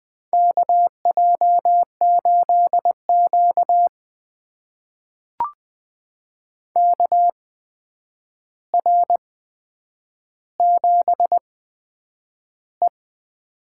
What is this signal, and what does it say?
Morse code, heard as a keyed tone, about 20 wpm, spaced as for 9 wpm, 700 Hz. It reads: KJ8Q KR7E
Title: USA Call Signs - 4 Characters - 8x Character Spacing - 20wpm